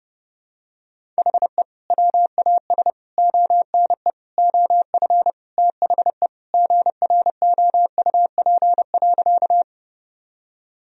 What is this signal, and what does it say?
Morse code, heard as a keyed tone, about 32 wpm, 700 Hz. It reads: HE WAS ONE OF THE GROUP.
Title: Sentences from Top 1000 Words - Review Part 1 - 30wpm